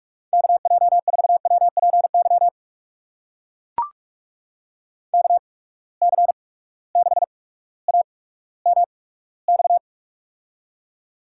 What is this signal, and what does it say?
Morse code, heard as a keyed tone, about 44 wpm, spaced as for 19 wpm, 700 Hz. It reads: KJ4WPY KC6AMX
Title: USA Call Signs - 8x Character Spacing - 45wpm